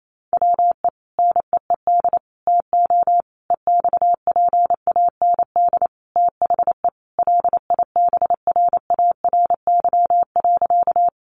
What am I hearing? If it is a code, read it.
Morse code, heard as a keyed tone, about 28 wpm, 700 Hz. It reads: WE NEED TO EXPAND THE LIBRARY.
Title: Sentences from Top 1400 Words 28wpm